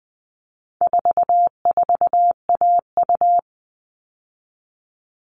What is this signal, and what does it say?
Morse code, heard as a keyed tone, about 20 wpm, 700 Hz. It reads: 44AU
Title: All Cut Numbers - 4-Digit 20wpm